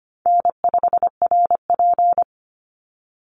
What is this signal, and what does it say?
Morse code, heard as a keyed tone, about 25 wpm, 700 Hz. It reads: N5RP
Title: Easy Call Signs - 25wpm